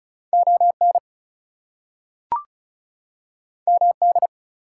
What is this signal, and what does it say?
Morse code, heard as a keyed tone, about 36 wpm, 700 Hz. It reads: ON MD
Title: US States and Canadian Provinces 35wpm